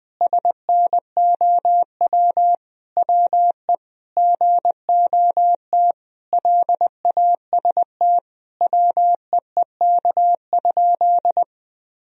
Morse code, heard as a keyed tone, about 20 wpm, 700 Hz. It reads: SNOW WE GOT LAST WEEK?
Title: Sentences from Top 1000 Words - Review Part 3 - 20wpm